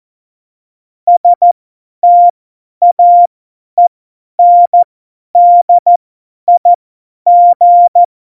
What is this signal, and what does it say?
Morse code, heard as a keyed tone, about 14 wpm, spaced as for 10 wpm, 700 Hz. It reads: STAENDIG
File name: Morsekurs / Witze / Witz 08. Januar 2019, 50 F70